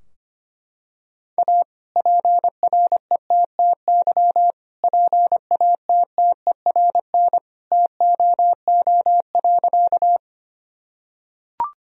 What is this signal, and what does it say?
Morse code, heard as a keyed tone, about 25 wpm, 700 Hz. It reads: A PRETTY PATTERN TOO.
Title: Sentences from Top 1000 Words - Review Part 2 - 25wpm